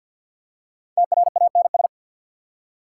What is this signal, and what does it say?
Morse code, heard as a keyed tone, about 50 wpm, 690 Hz. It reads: TRANS
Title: Prefixes and Suffixes 50wpm